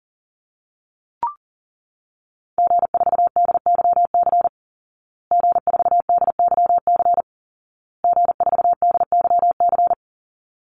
Morse code, heard as a keyed tone, about 40 wpm, 700 Hz. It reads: G4DYC G4DYC G4DYC
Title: Call Signs Repeated 2x - 40wpm